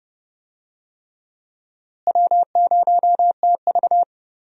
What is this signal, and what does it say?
Morse code, heard as a keyed tone, about 30 wpm, 700 Hz. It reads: W0TV
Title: USA Call Signs - 4 Characters - Repeated 3x - 30wpm